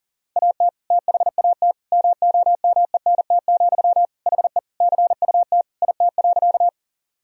Morse code, heard as a keyed tone, about 37 wpm, 700 Hz. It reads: AT THAT MOMENT, HE CUT IT.
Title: Sentences from Top 1000 Words - Review Part 1 - 40wpm